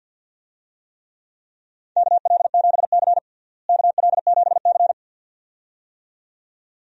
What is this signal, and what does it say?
Morse code, heard as a keyed tone, about 52 wpm, 700 Hz. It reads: KL7C KL7C
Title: DXCC-100 Call Signs - 4 Characters - Repeated 2x - 50wpm